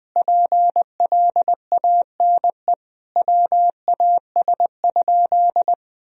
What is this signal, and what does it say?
Morse code, heard as a keyed tone, about 21 wpm, 700 Hz. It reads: PLANE WAS?
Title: Sentences from Top 1000 Words - Review Part 3 - 20wpm